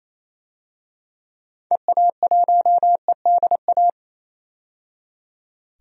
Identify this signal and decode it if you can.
Morse code, heard as a keyed tone, about 28 wpm, 700 Hz. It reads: EA1EDA
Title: DXCC-100 Call Signs - 6 Characters - 28wpm